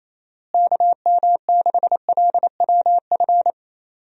Morse code, heard as a keyed tone, about 28 wpm, 700 Hz. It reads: KM6LWF
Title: USA Call Signs - 6 Characters - Repeated 2x - 28wpm